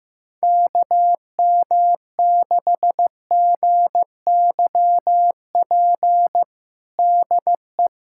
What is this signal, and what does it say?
Morse code, heard as a keyed tone, about 15 wpm, 700 Hz. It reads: KM6GYP DE
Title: USA Callsign DE Callsign - 6 Characters - 15wpm